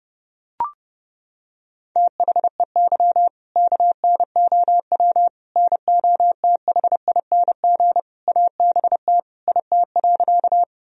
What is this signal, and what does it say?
Morse code, heard as a keyed tone, about 32 wpm, 700 Hz. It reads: THEY KNOW NOTHING ABT IT.
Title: Sentences from Top 500 Words and Abbreviations 30wpm